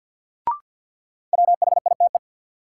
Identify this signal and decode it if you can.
Morse code, heard as a keyed tone, about 48 wpm, 700 Hz. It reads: WHITE
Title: Sets of 2 Words - New Word Drill - 50wpm